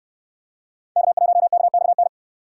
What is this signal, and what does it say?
Morse code, heard as a keyed tone, about 69 wpm, 700 Hz. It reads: K1KXN